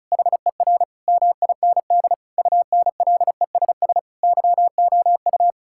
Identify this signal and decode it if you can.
Morse code, heard as a keyed tone, about 36 wpm, 700 Hz. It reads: HER MIND UNLESS YOU
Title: Sentences from Top 1000 Words - Review Part 6 - 35wpm